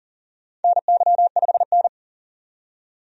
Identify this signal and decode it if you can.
Morse code, heard as a keyed tone, about 40 wpm, 700 Hz. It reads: NY5N
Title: USA Call Signs - 4 Characters - 40wpm